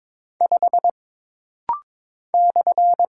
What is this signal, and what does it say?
Morse code, heard as a keyed tone, about 22 wpm, 700 Hz. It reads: 5 /
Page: This rapid-fire timing is geared to push your ICR (Instant Character Recognition) skills.